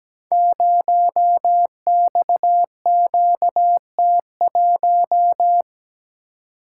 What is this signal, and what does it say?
Morse code, heard as a keyed tone, about 17 wpm, 700 Hz. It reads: 0XQT1
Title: Code Groups - 5 Letter:Numbers - 17wpm